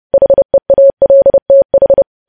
Morse code, heard as a keyed tone, about 30 wpm, 550 Hz. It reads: HEALTH
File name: podcast / News headlines-1x-30x30